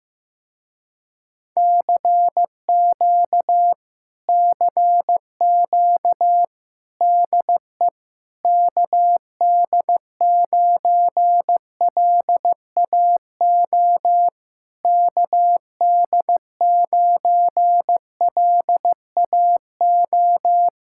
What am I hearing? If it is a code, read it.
Morse code, heard as a keyed tone, about 15 wpm, 700 Hz. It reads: CQ CQ DE KD9LAO KD9LAO
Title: Speed-Racing Common QSO Phrases Encore - 15wpm